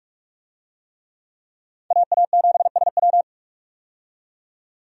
Morse code, heard as a keyed tone, about 45 wpm, 700 Hz. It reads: AA7SW
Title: USA Call Signs - 5 Characters - 45wpm